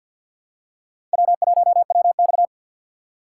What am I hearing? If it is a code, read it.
Morse code, heard as a keyed tone, about 52 wpm, 700 Hz. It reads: W1WX
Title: USA Call Signs - 4 Characters - 3x Character Spacing - 50wpm